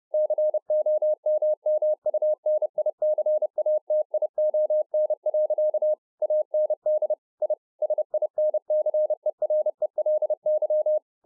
Morse code, heard as a keyed tone, about 30 wpm, 610 Hz. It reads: COMMUNICATION. AND I SINCERELY